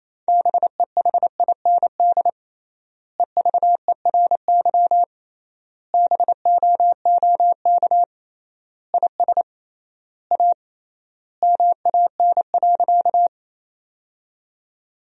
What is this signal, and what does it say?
Morse code, heard as a keyed tone, about 28 wpm, 700 Hz. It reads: BEHIND EVERY BOOK IS A MAN.
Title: 3x Word-Spacing - Sentences from Top 500 Words - 28wpm